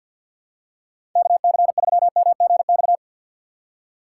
Morse code, heard as a keyed tone, about 52 wpm, 690 Hz. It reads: KC3MGX
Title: USA Call Signs - 6 Characters - 3x Character Spacing - 50wpm